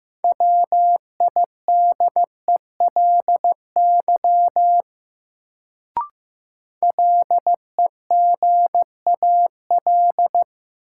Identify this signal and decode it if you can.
Morse code, heard as a keyed tone, about 15 wpm, 700 Hz. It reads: WIDELY LEGAL
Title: Sentences from Top 1200 Words - New Word Drill - 15wpm